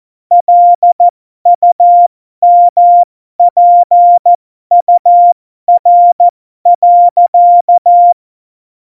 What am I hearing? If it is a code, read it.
Morse code, heard as a keyed tone, about 14 wpm, spaced as for 12 wpm, 700 Hz. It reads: LUMPUR.
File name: Morsekurs / Witze